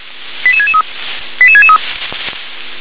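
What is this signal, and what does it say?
Five-tone selective call (ZVEI), heard as quick runs of five tones, tones of about 80 ms. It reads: 88663 88663
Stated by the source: Five-tone ZVEI selective call received on SDR.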